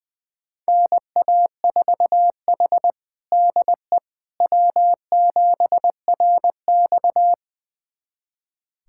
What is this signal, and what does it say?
Morse code, heard as a keyed tone, about 20 wpm, 700 Hz. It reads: NA4H DE W7RX
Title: USA Callsign DE Callsign - 4 Characters - 20wpm